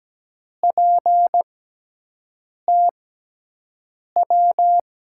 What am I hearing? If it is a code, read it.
Morse code, heard as a keyed tone, about 17 wpm, spaced as for 6 wpm, 700 Hz. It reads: PTW
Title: USA Call Signs - 6x Character Spacing - 17wpm